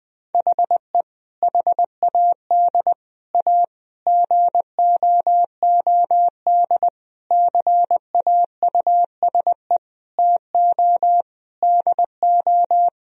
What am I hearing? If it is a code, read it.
Morse code, heard as a keyed tone, about 20 wpm, 700 Hz. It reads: HE HAD A GOOD CAUSE TO DO
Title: Sentences from Top 1000 Words - Review Part 1 - 20wpm